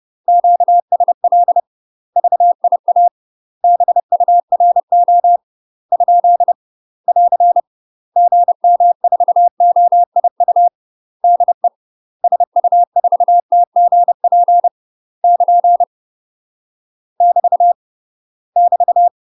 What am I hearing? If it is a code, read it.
Morse code, heard as a keyed tone, about 30 wpm, 700 Hz. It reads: QSL VIA BURO ? <AR> GM4OIU DE SU4TGP <KN> <BT> <BT>
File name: UK MASTER QSOs TRAINING / 30WPM / UK MASTER QSOs 0033